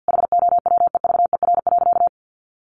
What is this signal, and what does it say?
Morse code, heard as a keyed tone, about 50 wpm, 700 Hz. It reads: HOWEVER.
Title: Sentences from Top 1100 Words 50wpm